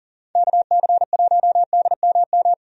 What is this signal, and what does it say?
Morse code, heard as a keyed tone, about 40 wpm, 700 Hz. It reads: KC1DMM
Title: USA Call Signs - 6 Characters - 40wpm